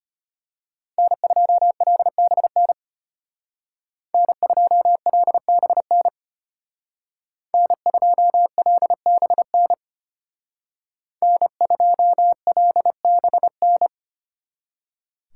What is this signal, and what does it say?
Morse code, heard as a keyed tone, about 31 wpm, 700 Hz. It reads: N2LBN N2LBN N2LBN N2LBN
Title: Speed Racing - USA 1x3 Call Signs 25wpm